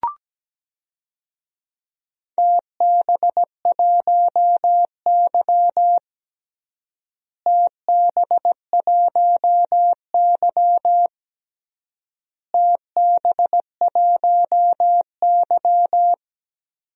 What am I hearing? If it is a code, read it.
Morse code, heard as a keyed tone, about 17 wpm, 700 Hz. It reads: TB1Y TB1Y TB1Y
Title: DXCC-100 Call Signs - 4 Characters - Repeated 3x - 17wpm